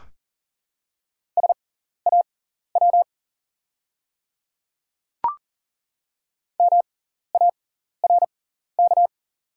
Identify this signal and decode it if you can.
Morse code, heard as a keyed tone, about 40 wpm, spaced as for 14 wpm, 690 Hz. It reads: SAW MARK
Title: 6x Character Spacing - Top 300 Words - 40wpm